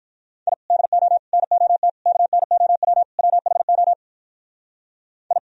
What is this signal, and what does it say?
Morse code, heard as a keyed tone, about 43 wpm, 700 Hz. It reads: I DO NOT KNOW WHO I
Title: Speed-Racing - Sets of 5 Words 35wpm